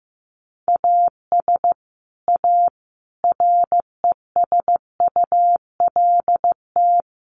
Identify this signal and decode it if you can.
Morse code, heard as a keyed tone, about 15 wpm, 700 Hz. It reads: AS A RESULT
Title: Sets of 5 Words Encore 15wpm